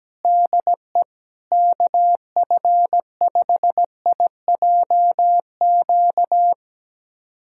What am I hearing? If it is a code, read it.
Morse code, heard as a keyed tone, about 17 wpm, 700 Hz. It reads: DE KF5IJQ
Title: USA Callsign DE Callsign - 6 Characters - 17wpm